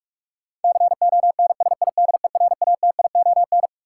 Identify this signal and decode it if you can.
Morse code, heard as a keyed tone, about 46 wpm, 700 Hz. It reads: CONSIDERATION
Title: Sentences from Top 1500 Words - New Word Drill - 45wpm